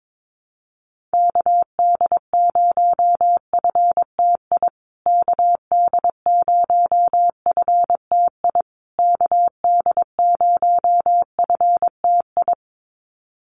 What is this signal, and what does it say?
Morse code, heard as a keyed tone, about 22 wpm, 700 Hz. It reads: KD0FTI KD0FTI KD0FTI
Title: USA Call Signs Repeated 3x 22wpm